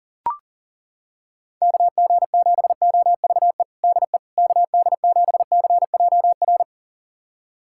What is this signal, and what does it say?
Morse code, heard as a keyed tone, about 37 wpm, 700 Hz. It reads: KG7OVE DE KD7CJR
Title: USA Callsign DE Callsign 40wpm